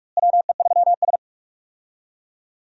Morse code, heard as a keyed tone, about 46 wpm, 700 Hz. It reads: WE3S
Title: USA Call Signs - 4 Characters - Repeated 2x - 45wpm